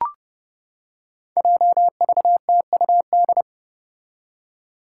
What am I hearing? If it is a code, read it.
Morse code, heard as a keyed tone, about 32 wpm, 700 Hz. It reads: JVTUD
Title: Five Challenging Letter-Number Combinations 30wpm